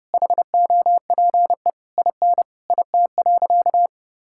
Morse code, heard as a keyed tone, about 30 wpm, 700 Hz. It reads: HOPE IN IT.
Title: Sentences from Top 1600 Words 30wpm